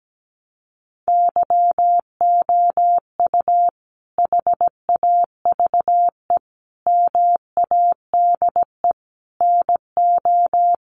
Morse code, heard as a keyed tone, about 17 wpm, 700 Hz. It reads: YOU HAVE MADE NO
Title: Sentences from Top 1400 Words 17wpm